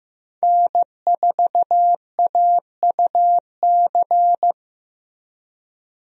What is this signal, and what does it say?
Morse code, heard as a keyed tone, about 15 wpm, 700 Hz. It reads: N4AUC
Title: USA Call Signs - 5 Characters - Repeated 2x - 15wpm